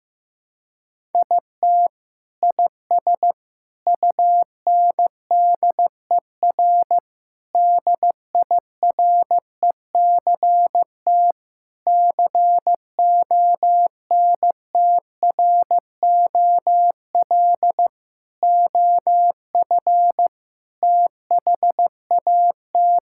Morse code, heard as a keyed tone, about 16 wpm, 700 Hz. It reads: IT IS UNDER DIRECT CONTROL OF THAT
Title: Sentences from Top 1000 Words - Review Part 5 - 15wpm